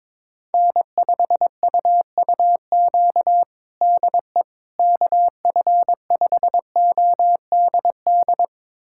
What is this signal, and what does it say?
Morse code, heard as a keyed tone, about 22 wpm, 700 Hz. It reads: N5UUQ DE KF5ODD
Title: USA Callsign DE Callsign 22wpm